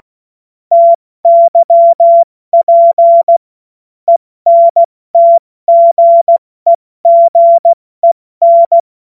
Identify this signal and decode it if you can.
Morse code, heard as a keyed tone, about 16 wpm, spaced as for 14 wpm, 680 Hz. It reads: TYP ENTGEGEN
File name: Morsekurs / Witze / Witz 29. Juni 2021, 70 F80